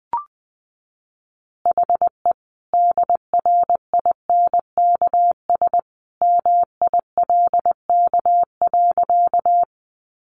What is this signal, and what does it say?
Morse code, heard as a keyed tone, about 20 wpm, 700 Hz. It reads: HE DRINKS MILK.
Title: Easy Phrases 20wpm